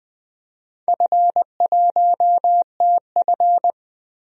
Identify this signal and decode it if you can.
Morse code, heard as a keyed tone, about 20 wpm, 700 Hz. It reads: F1TF
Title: DXCC-100 Call Signs - 4 Characters - 20wpm